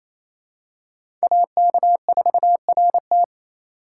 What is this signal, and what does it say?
Morse code, heard as a keyed tone, about 28 wpm, 700 Hz. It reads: AK4RT
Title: USA Call Signs - 5 Characters - 2x Character Spacing - 28wpm